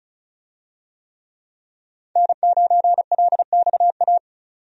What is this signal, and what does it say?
Morse code, heard as a keyed tone, about 36 wpm, 700 Hz. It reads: N9LXA
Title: USA Call Signs - 5 Characters - 35wpm